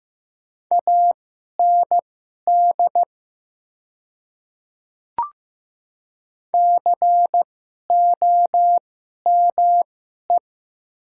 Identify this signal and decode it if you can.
Morse code, heard as a keyed tone, about 15 wpm, spaced as for 11 wpm, 700 Hz. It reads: AND COME
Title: 2x Character Spacing - Top 500 Words - 15wpm